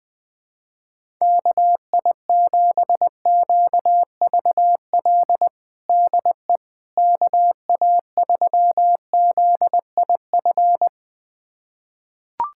Morse code, heard as a keyed tone, about 20 wpm, 700 Hz. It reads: KI7QVL DE KA3ZIF
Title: USA Callsign DE Callsign 20wpm